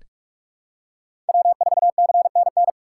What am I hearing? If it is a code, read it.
Morse code, heard as a keyed tone, about 46 wpm, 700 Hz. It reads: W4CNN
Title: USA Call Signs - 5 Characters - 4x Character Spacing - 45wpm